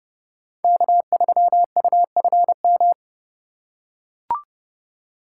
Morse code, heard as a keyed tone, about 29 wpm, 700 Hz. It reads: K3UFM